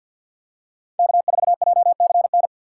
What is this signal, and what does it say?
Morse code, heard as a keyed tone, about 50 wpm, 700 Hz. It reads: K4JCN
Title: USA Call Signs - 5 Characters - 6x Character Spacing - 50wpm